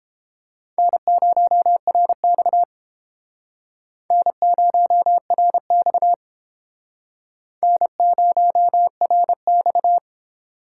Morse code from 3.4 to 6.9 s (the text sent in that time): N0RX